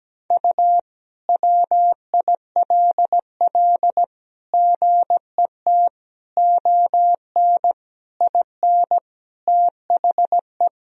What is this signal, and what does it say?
Morse code, heard as a keyed tone, about 17 wpm, 700 Hz. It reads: U WILL GET ON IN THE
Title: Sentences from Top 200 Words and Abbreviations 17wpm